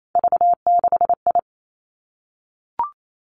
Morse code, heard as a keyed tone, about 28 wpm, 700 Hz. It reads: V6I
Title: Three Challenging Letter-Number Combinations 28wpm